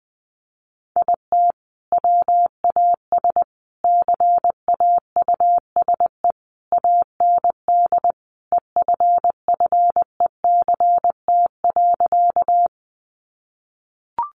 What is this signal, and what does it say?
Morse code, heard as a keyed tone, about 19 wpm, 700 Hz. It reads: IT WAS CAUSE AND EFFECT.
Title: Sentences from Top 1000 Words 20wpm